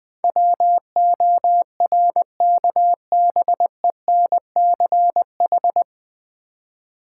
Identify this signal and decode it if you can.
Morse code, heard as a keyed tone, about 20 wpm, 690 Hz. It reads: WORKBENCH
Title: Difficult Words with Hints - 20wpm